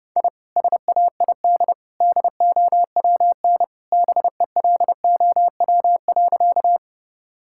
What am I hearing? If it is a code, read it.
Morse code, heard as a keyed tone, about 31 wpm, 700 Hz. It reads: I SAID DOWN BELOW.
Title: Sentences from Top 1000 Words - Review Part 5 - 30wpm